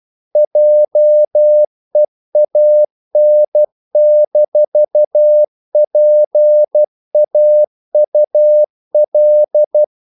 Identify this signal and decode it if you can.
Morse code, heard as a keyed tone, about 12 wpm, 600 Hz. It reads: JEAN-PAUL
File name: CW / Practice / EnglishQuotations0312